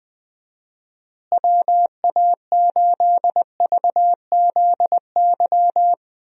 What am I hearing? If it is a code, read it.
Morse code, heard as a keyed tone, about 20 wpm, 700 Hz. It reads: WA8VZY
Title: USA Call Signs - 6 Characters - Repeated 3x - 20wpm